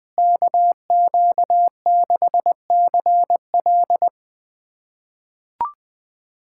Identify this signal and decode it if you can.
Morse code, heard as a keyed tone, about 20 wpm, 700 Hz. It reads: KQ6CL
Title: USA Callsign DE Callsign - 5 Characters - 20wpm